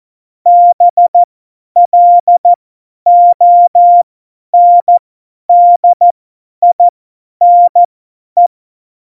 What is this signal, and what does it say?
Morse code, heard as a keyed tone, about 14 wpm, spaced as for 10 wpm, 700 Hz. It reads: BLONDINE
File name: Morsekurs / Witze / Witz 18 Juli 17, Tempo 50 F70